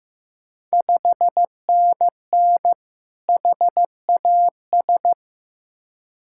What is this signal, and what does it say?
Morse code, heard as a keyed tone, about 15 wpm, 700 Hz. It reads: 5NN HAS
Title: Ontario QSO Party Exchange 15wpm